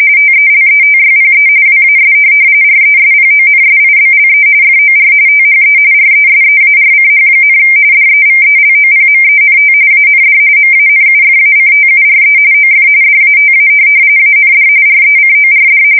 RTTY, about 45.45 baud. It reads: TELETYPE WAS RESPONSIBLE FOR THE RESEARCH, DEVELOPMENT AND MANUFACTURE OF DATA AND RECORD
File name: fsk170 / teletype-corp